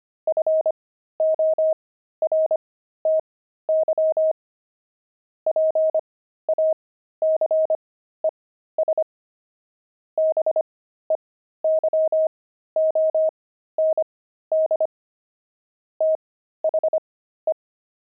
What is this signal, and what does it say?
Morse code, heard as a keyed tone, about 25 wpm, spaced as for 13 wpm, 630 Hz. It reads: FORTY PACES BEYOND THE
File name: pangram-fs-13-cs-25-javelins